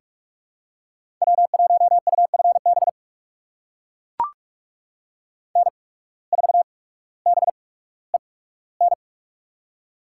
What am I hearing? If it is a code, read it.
Morse code, heard as a keyed tone, about 42 wpm, spaced as for 19 wpm, 700 Hz. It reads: W1UFB N4BEN
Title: USA Call Signs - 5 Characters - 8x Character Spacing - 45wpm